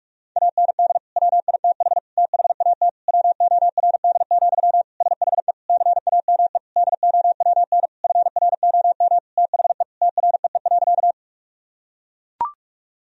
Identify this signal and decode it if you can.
Morse code, heard as a keyed tone, about 44 wpm, 700 Hz. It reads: AND WITH THAT WORD, SHE CAME DOWN FROM THE TREE.
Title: Sentences from Top 1000 Words - Review Part 1 - 45wpm